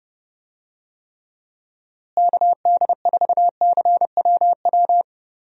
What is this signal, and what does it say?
Morse code, heard as a keyed tone, about 30 wpm, 700 Hz. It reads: KD4CWW
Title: USA Call Signs - 6 Characters - Repeated 3x - 30wpm